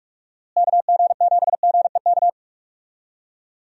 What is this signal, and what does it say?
Morse code, heard as a keyed tone, about 46 wpm, 700 Hz. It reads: KG7GEK
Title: USA Call Signs - 6 Characters - Repeated 2x - 45wpm